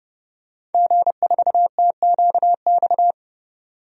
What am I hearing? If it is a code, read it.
Morse code, heard as a keyed tone, about 30 wpm, 700 Hz. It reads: G4TQX
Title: DXCC-100 Call Signs - 5 Characters - 30wpm